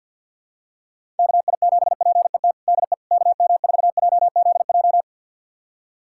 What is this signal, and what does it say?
Morse code, heard as a keyed tone, about 52 wpm, 700 Hz. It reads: KI7PET DE KM4JZJ
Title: USA Callsign DE Callsign - 6 Characters - 50wpm